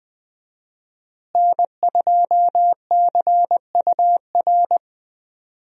Morse code, heard as a keyed tone, about 20 wpm, 700 Hz. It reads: N2CUR
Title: USA Call Signs - 5 Characters - Repeated 3x - 20wpm